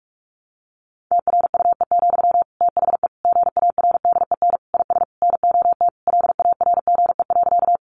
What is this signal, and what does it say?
Morse code, heard as a keyed tone, about 44 wpm, 700 Hz. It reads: TRUE, THE GARDEN IS NOT LARGE.
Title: Sentences from Top 700 Words 45wpm